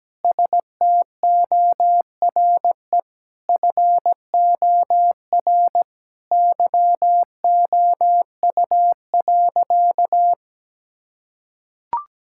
Morse code, heard as a keyed tone, about 17 wpm, 700 Hz. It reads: STORE FOR YOU.
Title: Sentences from Top 1000 Words - Review Part 5 - 17wpm